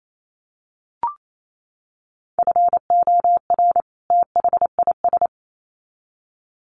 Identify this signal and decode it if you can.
Morse code, heard as a keyed tone, about 28 wpm, 700 Hz. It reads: FOR THIS
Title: Sets of 2 Words 28wpm